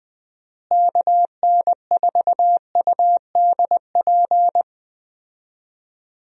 Morse code, heard as a keyed tone, about 20 wpm, 700 Hz. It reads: KN4UDP
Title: USA Call Signs - 6 Characters - Repeated 3x - 20wpm